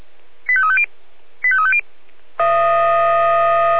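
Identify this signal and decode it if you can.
Five-tone selective call (ZVEI), heard as quick runs of five tones, tones of about 70 ms. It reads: 85350 85350
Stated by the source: ZVEI five-tone selective call received by SDR.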